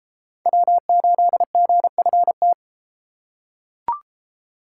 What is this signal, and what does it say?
Morse code, heard as a keyed tone, about 29 wpm, 700 Hz. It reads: W8GFT